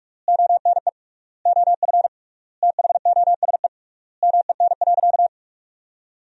Morse code, heard as a keyed tone, about 41 wpm, 700 Hz. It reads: ONE OF THOSE MEN.
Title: 3x Word-Spacing - Sentences from Top 300 Words - 45wpm